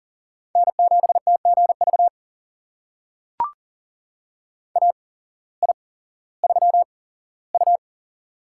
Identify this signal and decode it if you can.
Morse code, heard as a keyed tone, about 40 wpm, spaced as for 19 wpm, 700 Hz. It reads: N7TGV AI3U
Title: USA Call Signs - 8x Character Spacing - 40wpm